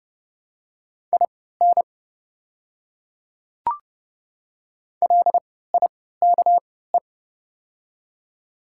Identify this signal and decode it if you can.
Morse code, heard as a keyed tone, about 30 wpm, spaced as for 17 wpm, 700 Hz. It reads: IN LIKE
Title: 3x Character Spacing - Top 100 Words - 30wpm